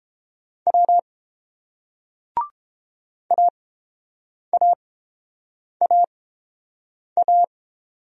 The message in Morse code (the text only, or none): W AAAA